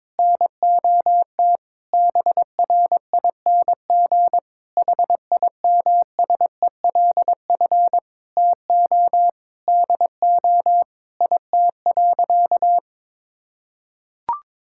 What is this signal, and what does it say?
Morse code, heard as a keyed tone, about 22 wpm, 700 Hz. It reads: NOT BRING HIMSELF TO DO IT.
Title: Sentences from Top 1600 Words 22wpm